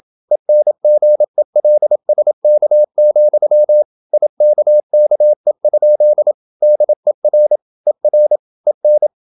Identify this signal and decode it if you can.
Morse code, heard as a keyed tone, about 27 wpm, 600 Hz. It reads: ENGELSK, IKKE? DER ER EN
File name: Ferie QRQ 27wpm